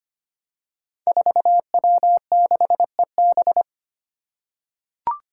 Morse code, heard as a keyed tone, about 25 wpm, 700 Hz. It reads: 4W6EB
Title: Five Challenging Letter-Number Combinations 25wpm